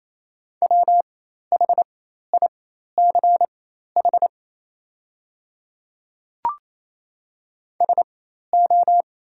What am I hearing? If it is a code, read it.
Morse code, heard as a keyed tone, about 28 wpm, spaced as for 13 wpm, 690 Hz. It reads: WHICH SO
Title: 4x Character Spacing - Top 100 Words - 28wpm